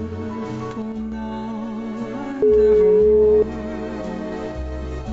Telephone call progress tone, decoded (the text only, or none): ringback tone